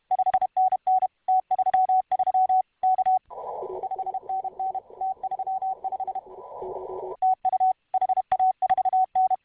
Morse code, heard as a keyed tone, about 32 wpm, 750 Hz. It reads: 5NN T33 K 5NN T35 TU HA4N